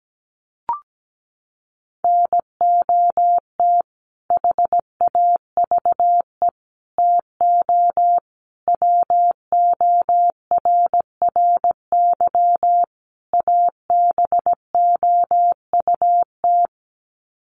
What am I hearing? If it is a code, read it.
Morse code, heard as a keyed tone, about 17 wpm, 700 Hz. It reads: NOT HAVE TO WORRY ABOUT
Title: Sets of 5 Words 17wpm